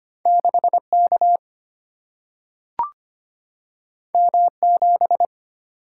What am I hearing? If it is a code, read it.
Morse code, heard as a keyed tone, about 25 wpm, 700 Hz. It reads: 6 K M 7
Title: Single-Letter Single-Number 25wpm